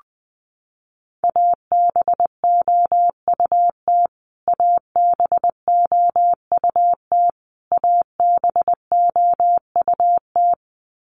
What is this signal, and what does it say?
Morse code, heard as a keyed tone, about 20 wpm, 700 Hz. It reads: ABOUT ABOUT ABOUT
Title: Top 200 Words Repeated 20wpm